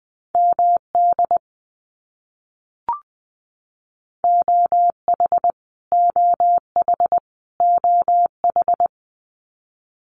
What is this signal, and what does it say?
Morse code, heard as a keyed tone, about 20 wpm, 700 Hz. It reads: MD OH OH OH
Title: US States Repeated 3x 20wpm